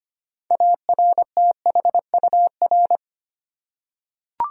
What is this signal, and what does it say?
Morse code, heard as a keyed tone, about 25 wpm, 700 Hz. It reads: ARTHUR
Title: USA Common Names - 25wpm